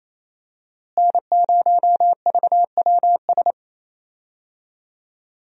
Morse code, heard as a keyed tone, about 28 wpm, 700 Hz. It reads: N0VWS
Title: USA Call Signs - 5 Characters - Repeated 3x - 28wpm